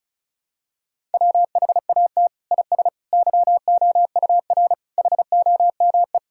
Morse code, heard as a keyed tone, about 36 wpm, 700 Hz. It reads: WHAT IS YOUR HOME
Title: Sentences from Top 1000 Words - Review Part 6 - 35wpm